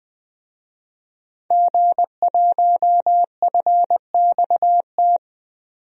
Morse code, heard as a keyed tone, about 20 wpm, 700 Hz. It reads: G1FXT
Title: DXCC-100 Call Signs - 5 Characters - 20wpm